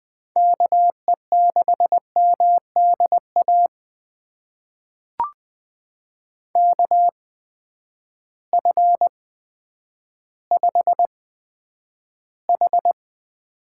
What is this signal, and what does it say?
Morse code, heard as a keyed tone, about 20 wpm, spaced as for 10 wpm, 700 Hz. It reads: KE6MDA KF5H
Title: USA Call Signs - 6 Characters - 8x Character Spacing - 20wpm